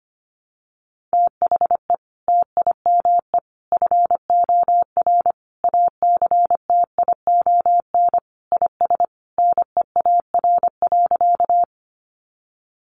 Morse code, heard as a keyed tone, about 25 wpm, 700 Hz. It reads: THE TIME FOR ACTION IS NEAR.
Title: Sentences from Top 700 Words 25wpm